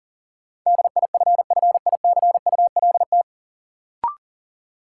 Morse code, heard as a keyed tone, about 40 wpm, 700 Hz. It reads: DIFFICULT
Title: Sets of 4 Words - New Word Drill - 40wpm